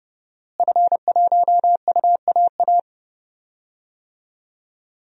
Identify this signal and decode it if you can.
Morse code, heard as a keyed tone, about 30 wpm, 700 Hz. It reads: F1UAA
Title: DXCC-100 Call Signs - 5 Characters - Repeated 3x - 30wpm